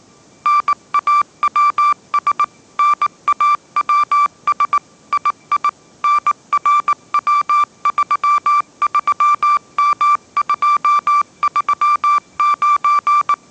Morse code, heard as a keyed tone, about 19 wpm, 1170 Hz. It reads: NAWS NAWS II NRW33M239
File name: mrx083 20wpm